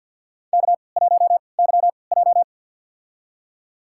Morse code, heard as a keyed tone, about 52 wpm, spaced as for 29 wpm, 700 Hz. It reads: K1YJ
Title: USA Call Signs - 4 Characters - 3x Character Spacing - 50wpm